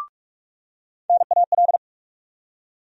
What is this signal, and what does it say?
Morse code, heard as a keyed tone, about 46 wpm, 700 Hz. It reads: NAL
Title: USA Call Sign Postfixes - 3 Letters 45wpm